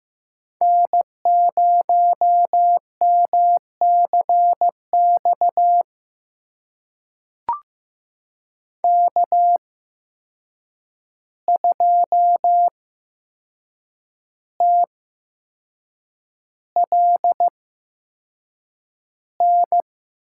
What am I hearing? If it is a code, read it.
Morse code, heard as a keyed tone, about 15 wpm, spaced as for 6 wpm, 700 Hz. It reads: N0MCX K2TLN
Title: USA Call Signs - 5 Characters - 8x Character Spacing - 15wpm